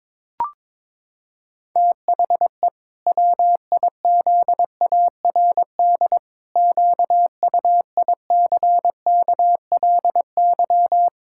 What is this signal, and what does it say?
Morse code, heard as a keyed tone, about 22 wpm, 700 Hz. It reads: THE WIZARD QUICKLY
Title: Pangram Challenge v2 - 22wpm